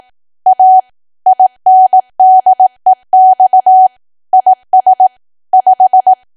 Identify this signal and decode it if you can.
Morse code, heard as a keyed tone, about 18 wpm, 750 Hz. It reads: A INDEX IS 5
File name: cwa / W1AW-Level2 / 18WPM